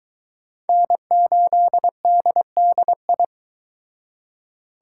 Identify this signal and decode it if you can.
Morse code, heard as a keyed tone, about 22 wpm, 700 Hz. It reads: N8DDI